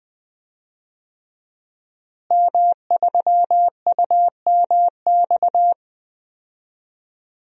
Morse code, heard as a keyed tone, about 20 wpm, 700 Hz. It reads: M3UMX
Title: DXCC-100 Call Signs - 5 Characters - Repeated 3x - 20wpm